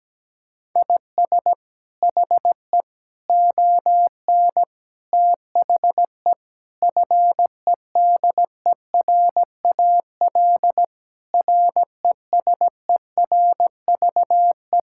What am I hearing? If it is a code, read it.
Morse code, heard as a keyed tone, about 17 wpm, 700 Hz. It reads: IS HE ON THE FEDERAL RESERVE
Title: Sentences from Top 1000 Words - Review Part 6 - 17wpm